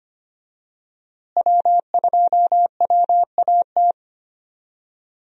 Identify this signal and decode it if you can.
Morse code, heard as a keyed tone, about 25 wpm, 700 Hz. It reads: W2WAT
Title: USA Call Signs - 5 Characters - Repeated 3x - 25wpm